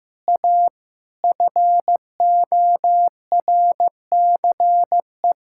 Morse code, heard as a keyed tone, about 15 wpm, 700 Hz. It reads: A FORCE
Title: Sentences from Top 1600 Words 15wpm